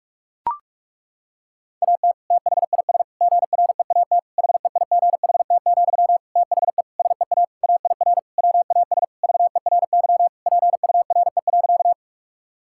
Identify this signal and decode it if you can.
Morse code, heard as a keyed tone, about 41 wpm, 700 Hz. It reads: AT THIS GREAT HEIGHT, THE SEA AIR WAS VERY PURE.
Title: Sentences from Top 1600 Words 45wpm